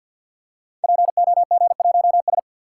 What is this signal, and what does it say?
Morse code, heard as a keyed tone, about 46 wpm, 700 Hz. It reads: POG1S
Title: Code Groups - 5 Letter:Numbers - 50wpm